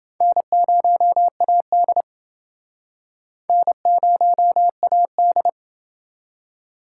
Morse code, heard as a keyed tone, about 29 wpm, 700 Hz. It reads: N0AD N0AD